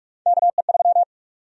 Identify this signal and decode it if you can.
Morse code, heard as a keyed tone, about 46 wpm, 700 Hz. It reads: KE3
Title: USA Call Sign Prefixes - 45wpm